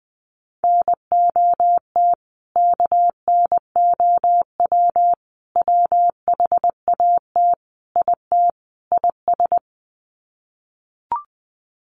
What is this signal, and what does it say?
Morse code, heard as a keyed tone, about 20 wpm, 700 Hz. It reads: NOT KNOW WHAT IT IS
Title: Sets of 5 Words Encore 20wpm